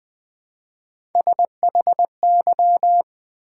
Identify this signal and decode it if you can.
Morse code, heard as a keyed tone, about 20 wpm, 700 Hz. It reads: SHY
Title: Build A Word - 1 Letter At A Time - 3-Letter Words - 20wpm